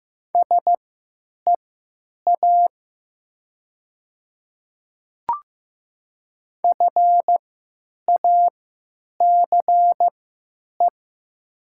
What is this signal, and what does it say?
Morse code, heard as a keyed tone, about 15 wpm, spaced as for 9 wpm, 700 Hz. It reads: SEA FACE
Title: 3x Character Spacing - Top 300 Words - 15wpm